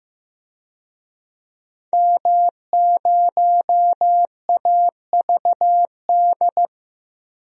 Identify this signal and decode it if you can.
Morse code, heard as a keyed tone, about 15 wpm, 700 Hz. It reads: M0AVD
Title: DXCC-100 Call Signs - 5 Characters - Repeated 2x - 15wpm